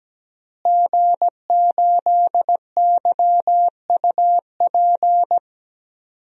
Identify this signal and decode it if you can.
Morse code, heard as a keyed tone, about 17 wpm, 700 Hz. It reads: G8YUP
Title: DXCC-100 Call Signs - 5 Characters - Repeated 2x - 17wpm